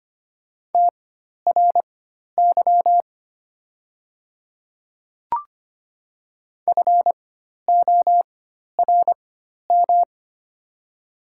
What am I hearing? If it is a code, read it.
Morse code, heard as a keyed tone, about 25 wpm, spaced as for 12 wpm, 700 Hz. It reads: TRY FORM
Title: 4x Character Spacing - Top 300 Words - 25wpm